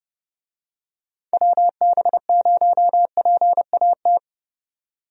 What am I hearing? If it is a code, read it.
Morse code, heard as a keyed tone, about 30 wpm, 700 Hz. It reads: WB0PAT
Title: USA Call Signs - 6 Characters - 30wpm